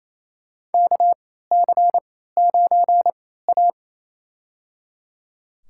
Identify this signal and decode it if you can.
Morse code, heard as a keyed tone, about 28 wpm, spaced as for 16 wpm, 700 Hz. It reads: KC9A
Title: USA Call Signs - 4 Characters - 3x Character Spacing - 28wpm